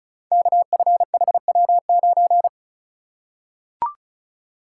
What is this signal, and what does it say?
Morse code, heard as a keyed tone, about 36 wpm, 700 Hz. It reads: KFHW9
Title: Code Groups - 5 Letter:Numbers - 35wpm